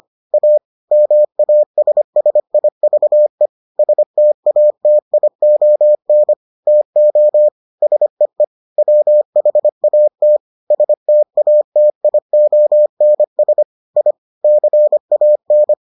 Morse code, heard as a keyed tone, about 25 wpm, 600 Hz. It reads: A MASSIVE STATION TO SEE WHAT STATIONS I CAN